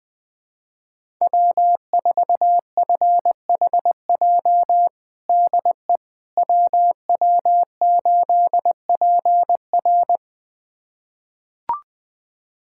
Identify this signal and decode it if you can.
Morse code, heard as a keyed tone, about 20 wpm, 700 Hz. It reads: W4FHJ DE WW8PR
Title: USA Callsign DE Callsign - 5 Characters - 20wpm